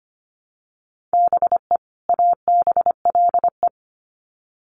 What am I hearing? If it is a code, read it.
Morse code, heard as a keyed tone, about 25 wpm, 700 Hz. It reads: BE ABLE
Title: Sets of 2 Words 25wpm